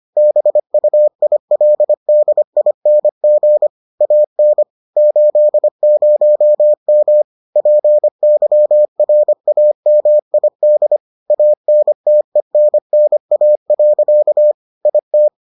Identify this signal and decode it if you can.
Morse code, heard as a keyed tone, about 25 wpm, 600 Hz. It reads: BUILDING AN 80M PYRAMID ANTENNA. IT